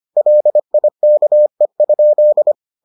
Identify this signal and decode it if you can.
Morse code, heard as a keyed tone, about 25 wpm, 600 Hz. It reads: LIKE?